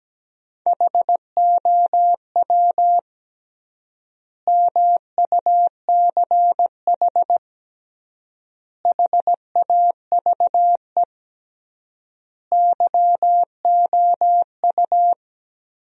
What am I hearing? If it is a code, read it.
Morse code, heard as a keyed tone, about 17 wpm, 700 Hz. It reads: HOW MUCH HAVE YOU
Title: 3x Word-Spacing - Sentences from Top 300 Words - 17wpm